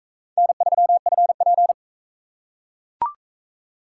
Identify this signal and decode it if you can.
Morse code, heard as a keyed tone, about 41 wpm, 700 Hz. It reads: N3FP